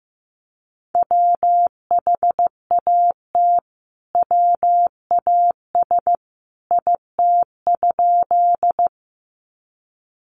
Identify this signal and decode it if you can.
Morse code, heard as a keyed tone, about 15 wpm, 700 Hz. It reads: WHAT WAS IT?
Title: Sentences from Top 100 Words 15wpm